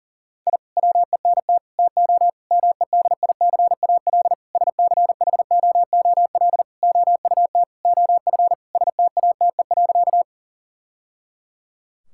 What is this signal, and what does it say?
Morse code, heard as a keyed tone, about 40 wpm, 700 Hz. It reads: I WENT TO MEDICAL SCHOOL OUT OF STATE.
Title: Sentences from Top 1000 Words - Review Part 5 - 40wpm